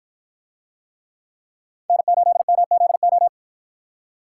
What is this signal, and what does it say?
Morse code, heard as a keyed tone, about 52 wpm, 700 Hz. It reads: N8MZO